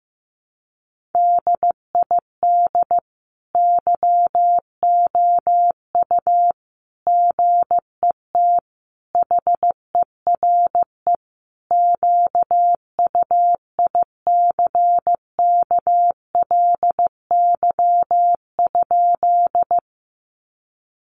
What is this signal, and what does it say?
Morse code, heard as a keyed tone, about 15 wpm, 700 Hz. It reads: DID YOU GET HERE QUICKLY?
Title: Sentences from Top 700 Words 15wpm